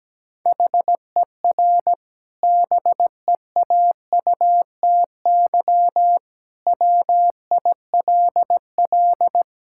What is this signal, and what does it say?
Morse code, heard as a keyed tone, about 17 wpm, 700 Hz. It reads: HER BEAUTY WILL
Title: Sentences from Top 1000 Words - Review Part 5 - 17wpm